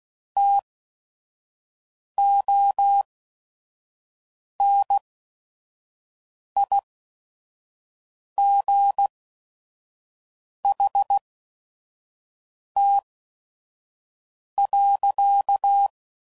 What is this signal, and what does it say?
Morse code, heard as a keyed tone, about 16 wpm, spaced as for 5 wpm, 800 Hz. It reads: TONIGHT.
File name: CW Code Groups @5WPM / QSOs / WD0FBT 1